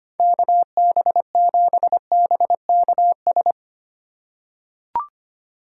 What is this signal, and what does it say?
Morse code, heard as a keyed tone, about 25 wpm, 700 Hz. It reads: KB7BKS
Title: USA Call Signs - 6 Characters - 4x Character Spacing - 25wpm